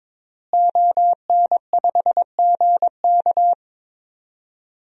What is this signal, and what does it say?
Morse code, heard as a keyed tone, about 22 wpm, 700 Hz. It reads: ON5GK
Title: DXCC-100 Call Signs - 5 Characters - Repeated 3x - 22wpm